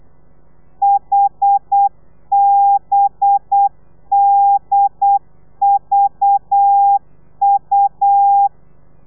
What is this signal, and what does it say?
Morse code, heard as a keyed tone, about 8 wpm, 800 Hz. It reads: HBDVU
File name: telegrafia / i2tfj / Lez4 35